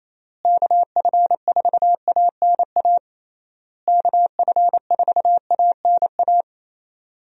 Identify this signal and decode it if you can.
Morse code, heard as a keyed tone, about 28 wpm, 700 Hz. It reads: KF4ANA KF4ANA
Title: USA Call Signs - 6 Characters - Repeated 3x - 28wpm